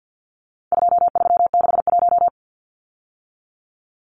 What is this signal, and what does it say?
Morse code, heard as a keyed tone, about 50 wpm, 700 Hz. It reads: 2361
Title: Numbers - 4-Digit 50wpm